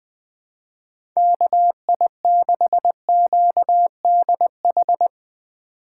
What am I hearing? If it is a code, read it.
Morse code, heard as a keyed tone, about 20 wpm, 700 Hz. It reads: KI6QDH
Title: K1USN SST QSO Practice - 20wpm